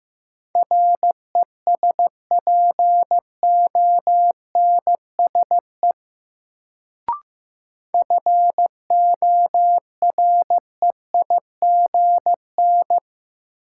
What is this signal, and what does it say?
Morse code, heard as a keyed tone, about 15 wpm, 700 Hz. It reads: RESPONSE FOREIGN
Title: Sentences from Top 1400 Words - New Word Drill - 15wpm